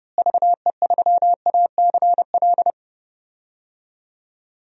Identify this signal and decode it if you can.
Morse code, heard as a keyed tone, about 30 wpm, 700 Hz. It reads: VE3ACL
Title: DXCC-100 Call Signs - 6 Characters - Repeated 3x - 30wpm